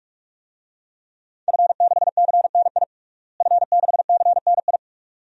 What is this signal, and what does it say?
Morse code, heard as a keyed tone, about 44 wpm, 700 Hz. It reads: F6CNI F6CNI
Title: DXCC-100 Call Signs - 5 Characters - Repeated 2x - 45wpm